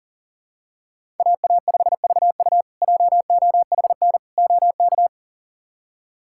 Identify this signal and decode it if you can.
Morse code, heard as a keyed tone, about 40 wpm, 700 Hz. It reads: AA5VU JOHN OK
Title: K1USN SST QSO Practice - 40wpm